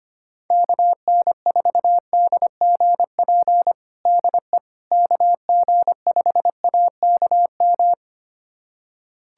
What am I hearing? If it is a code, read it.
Morse code, heard as a keyed tone, about 25 wpm, 700 Hz. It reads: KN4DGP DE KG5AKM
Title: USA Callsign DE Callsign - 6 Characters - 25wpm